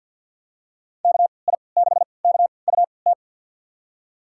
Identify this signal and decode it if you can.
Morse code, heard as a keyed tone, about 52 wpm, spaced as for 29 wpm, 700 Hz. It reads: KI6KUT
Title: USA Call Signs - 3x Character Spacing - 50wpm